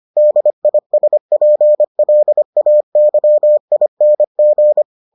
Morse code, heard as a keyed tone, about 25 wpm, 600 Hz. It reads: DISPLAYING